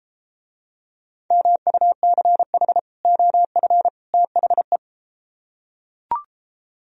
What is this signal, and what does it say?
Morse code, heard as a keyed tone, about 34 wpm, 700 Hz. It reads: MUCH OF THE